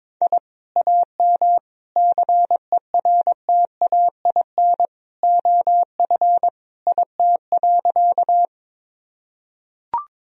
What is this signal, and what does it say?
Morse code, heard as a keyed tone, about 22 wpm, 700 Hz. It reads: I AM CERTAIN OF IT.
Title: Sentences from Top 1000 Words - Review Part 2 - 22wpm